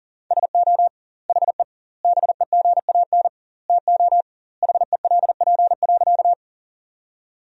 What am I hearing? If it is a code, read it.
Morse code, heard as a keyed tone, about 40 wpm, 700 Hz. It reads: SO HE BEGAN TO HELP.
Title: 2x Word-Spacing - Sentences from Top 300 Words - 40wpm